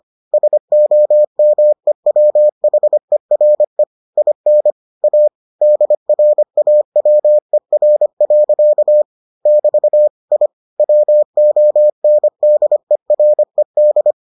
Morse code, heard as a keyed tone, about 25 wpm, 600 Hz. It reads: SOMEWHERE IN A DRAWER. <BT> I WONDERED